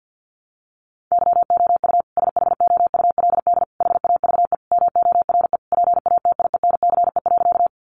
Morse code, heard as a keyed tone, about 50 wpm, 700 Hz. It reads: YOU SHOULD HAVE MORE PATIENCE.
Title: Sentences from Top 1300 Words 50wpm